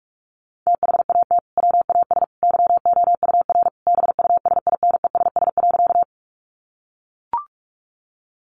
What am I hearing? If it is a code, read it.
Morse code, heard as a keyed tone, about 44 wpm, 700 Hz. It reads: THAT WAS YOUR BUSINESS.
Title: Sentences from Top 600 Words 45wpm